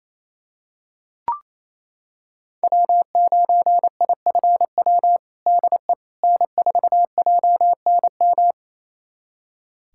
Morse code, heard as a keyed tone, about 28 wpm, 700 Hz. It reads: W9IFW DE N4JNM
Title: USA Callsign DE Callsign - 5 Characters - 28wpm